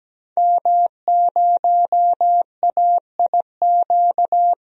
Morse code, heard as a keyed tone, about 17 wpm, 700 Hz. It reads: M0AIQ
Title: DXCC-100 Call Signs - 5 Characters - 17wpm